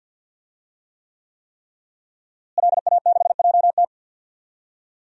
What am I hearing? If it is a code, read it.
Morse code, heard as a keyed tone, about 51 wpm, 700 Hz. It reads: LA6JT
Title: DXCC-100 Call Signs - 5 Characters - Repeated 3x - 50wpm